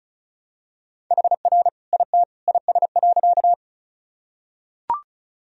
Morse code, heard as a keyed tone, about 35 wpm, 700 Hz. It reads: HR IT IS.
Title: Sentences from Top 200 Words and Abbreviations 35wpm